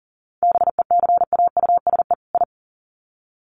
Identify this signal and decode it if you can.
Morse code, heard as a keyed tone, about 40 wpm, 700 Hz. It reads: BECAUSE I
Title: Sets of 2 Words 40wpm